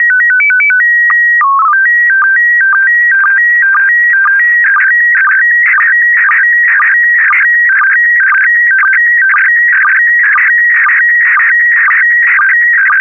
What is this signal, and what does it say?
SSTV (slow-scan television), PD 120.
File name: ISS-SSTV / SSTV-Audio-ISS-Bild-07